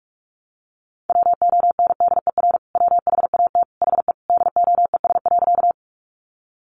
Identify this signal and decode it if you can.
Morse code, heard as a keyed tone, about 44 wpm, 700 Hz. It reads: WONDER WHAT HE DOES.
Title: Sentences from Top 500 Words 45wpm